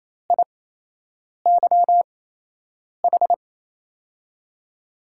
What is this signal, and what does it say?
Morse code, heard as a keyed tone, about 28 wpm, spaced as for 8 wpm, 700 Hz. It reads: IYH
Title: USA Call Signs - 5 Characters - 8x Character Spacing - 28wpm